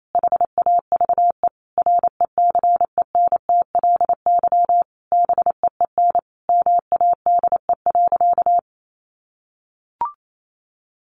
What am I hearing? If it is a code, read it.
Morse code, heard as a keyed tone, about 28 wpm, 700 Hz. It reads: HAVE RECENTLY BEEN MADE.
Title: Sentences from Top 1200 Words 28wpm